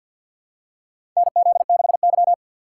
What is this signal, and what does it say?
Morse code, heard as a keyed tone, about 47 wpm, 700 Hz. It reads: NZ6Y
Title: USA Call Signs - 4 Characters - 3x Character Spacing - 50wpm